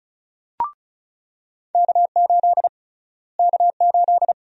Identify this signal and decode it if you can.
Morse code, heard as a keyed tone, about 35 wpm, 700 Hz. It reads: K8 K8
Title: USA Call Sign Prefixes - Repeated 2x - 35wpm